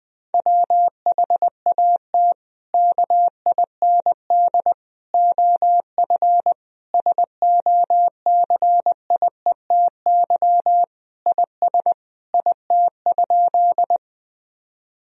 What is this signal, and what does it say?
Morse code, heard as a keyed tone, about 20 wpm, 700 Hz. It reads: WHAT KIND OF SOCIETY IS IT?
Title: Sentences from Top 1000 Words - Review Part 4 - 20wpm